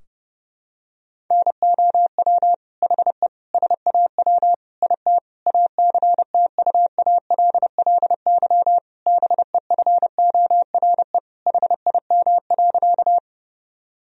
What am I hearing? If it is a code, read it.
Morse code, heard as a keyed tone, about 30 wpm, 700 Hz. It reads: NOW HE SAW IT ACTUALLY BEFORE HIM.
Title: Sentences from Top 1000 Words - Review Part 4 - 30wpm